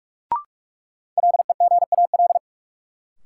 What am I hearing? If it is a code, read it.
Morse code, heard as a keyed tone, about 46 wpm, 700 Hz. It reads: LEGAL
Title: Sentences from Top 1200 Words - New Word Drill - 45wpm